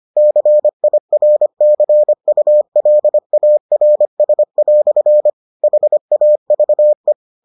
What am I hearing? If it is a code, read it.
Morse code, heard as a keyed tone, about 25 wpm, 600 Hz. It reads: CIRCULARS" HAVE